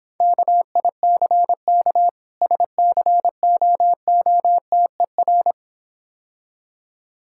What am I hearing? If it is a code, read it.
Morse code, heard as a keyed tone, about 26 wpm, 700 Hz. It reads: KICK SCOOTER